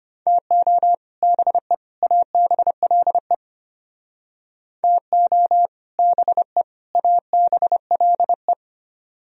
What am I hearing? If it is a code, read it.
Morse code, heard as a keyed tone, about 33 wpm, 700 Hz. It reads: TO BE ABLE TO BE ABLE